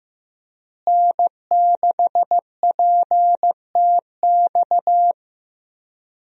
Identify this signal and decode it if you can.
Morse code, heard as a keyed tone, about 16 wpm, 700 Hz. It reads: N6PTX
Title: USA Call Signs - 5 Characters - Repeated 3x - 15wpm